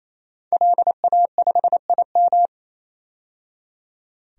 Morse code, heard as a keyed tone, about 28 wpm, 700 Hz. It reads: LA5IM
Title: DXCC-100 Call Signs - 5 Characters - 28wpm